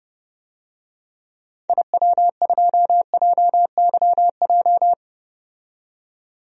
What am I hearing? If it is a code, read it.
Morse code, heard as a keyed tone, about 32 wpm, 700 Hz. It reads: IW2JYJ
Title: DXCC-100 Call Signs - 6 Characters - 30wpm